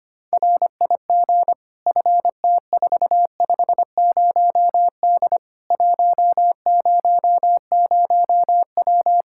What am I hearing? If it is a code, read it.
Morse code, heard as a keyed tone, about 25 wpm, 700 Hz. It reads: RIG FT450D 100W
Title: Practice QSO - Exchange 2 - Info - 25wpm